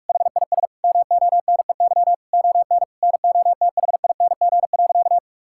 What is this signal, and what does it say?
Morse code, heard as a keyed tone, about 46 wpm, 700 Hz. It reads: HIS MONEY ON NOTHING.
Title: Sentences from Top 1000 Words - Review Part 5 - 45wpm